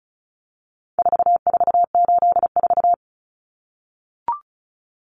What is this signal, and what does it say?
Morse code, heard as a keyed tone, about 34 wpm, 700 Hz. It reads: 4484
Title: Common Cut Numbers - 4-Digit 35wpm